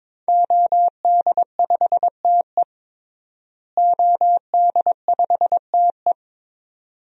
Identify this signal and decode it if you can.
Morse code, heard as a keyed tone, about 22 wpm, 700 Hz. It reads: OD5TE OD5TE
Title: DXCC-100 Call Signs - 5 Characters - Repeated 2x - 22wpm